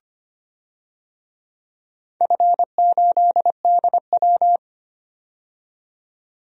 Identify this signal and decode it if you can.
Morse code, heard as a keyed tone, about 25 wpm, 700 Hz. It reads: F8DW
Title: DXCC-100 Call Signs - 4 Characters - 25wpm